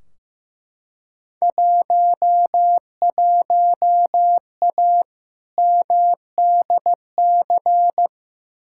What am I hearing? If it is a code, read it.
Morse code, heard as a keyed tone, about 15 wpm, 700 Hz. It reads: 11A MDC
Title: Field Day Exchanges Encore - 15wpm